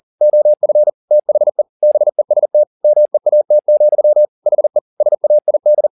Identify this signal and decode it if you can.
Morse code, heard as a keyed tone, about 40 wpm, 600 Hz. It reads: OF THE BEST MEAT, HE SAID
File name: the old man and the sea 2 40wpm